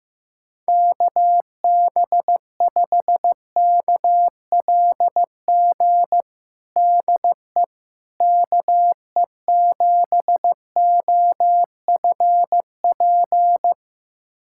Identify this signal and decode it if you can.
Morse code, heard as a keyed tone, about 15 wpm, 700 Hz. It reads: KB5KLG DE KE7OFP
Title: USA Callsign DE Callsign 15wpm